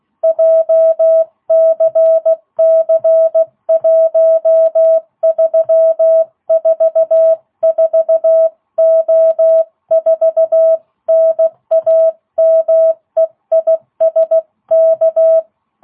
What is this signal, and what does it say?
Morse code, heard as a keyed tone, about 16 wpm, 660 Hz. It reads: JCC1344O4 NAME IS K